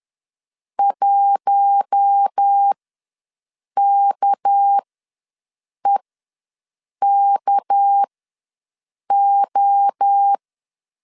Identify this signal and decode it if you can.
Morse code, heard as a keyed tone, about 10 wpm, spaced as for 6 wpm, 800 Hz. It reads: 1KEKO
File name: LEZIONI / 6 PKX0 / 6 PK ws 10wpm 3745